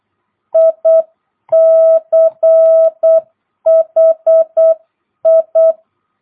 Morse code, heard as a keyed tone, about 8 wpm, 660 Hz. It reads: ICHI
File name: jh8cbh / 2022cw